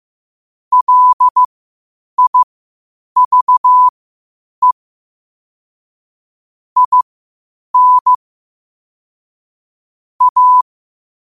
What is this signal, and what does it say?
Morse code, heard as a keyed tone, about 15 wpm, spaced as for 8 wpm, 1000 Hz. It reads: LIVE IN A
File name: qotdfiles / 8wpm